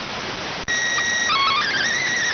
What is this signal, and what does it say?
SSTV (slow-scan television), PD 290.